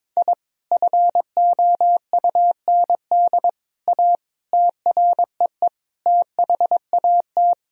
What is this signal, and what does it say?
Morse code, heard as a keyed tone, about 22 wpm, 700 Hz. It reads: I FOUND A TREE THAT
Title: Sentences from Top 1000 Words - Review Part 1 - 22wpm